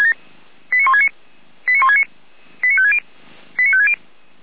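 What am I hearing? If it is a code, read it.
Five-tone selective call (ZVEI), heard as quick runs of five tones, tones of about 70 ms. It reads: 89169 89169 89560 89560
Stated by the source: ZVEI five-tone selective call 89223 received by SDR, from the feuerwehr-orlamuende server.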